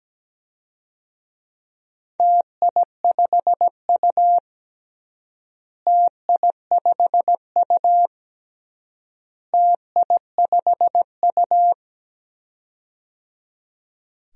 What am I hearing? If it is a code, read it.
Morse code, heard as a keyed tone, about 17 wpm, 700 Hz. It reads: TI5U TI5U TI5U
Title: DXCC-100 Call Signs - 4 Characters - Repeated 3x - 17wpm